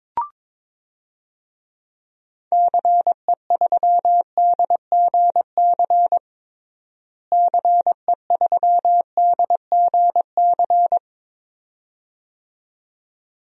Morse code, heard as a keyed tone, about 22 wpm, 700 Hz. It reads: CE3DGC CE3DGC
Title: DXCC-100 Call Signs - 6 Characters - Repeated 2x - 22wpm